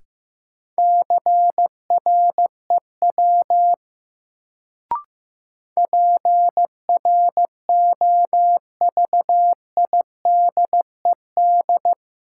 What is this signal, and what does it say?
Morse code, heard as a keyed tone, about 15 wpm, 700 Hz. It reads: CREW PROVIDED
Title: Sentences from Top 1500 Words - New Word Drill Alt - 15wpm